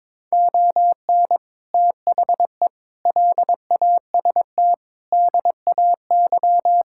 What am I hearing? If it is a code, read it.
Morse code, heard as a keyed tone, about 22 wpm, 700 Hz. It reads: ON THE LAST DAY
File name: practice / 22wpm / Sets of 6 Words - 22wpm